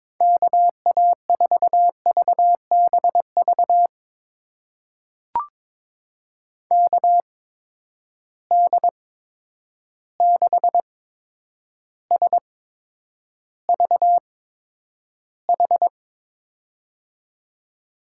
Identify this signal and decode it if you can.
Morse code, heard as a keyed tone, about 23 wpm, spaced as for 10 wpm, 700 Hz. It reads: KA4VBV KD6SVH
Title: USA Call Signs - 6 Characters - 8x Character Spacing - 22wpm